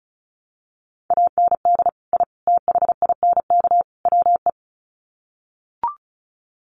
Morse code, heard as a keyed tone, about 34 wpm, 700 Hz. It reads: AND I THINK WE
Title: Sets of 4 Words 35wpm